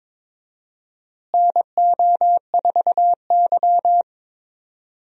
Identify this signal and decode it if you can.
Morse code, heard as a keyed tone, about 22 wpm, 700 Hz. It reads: NO4Y
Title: USA Call Signs - 4 Characters - Repeated 2x - 22wpm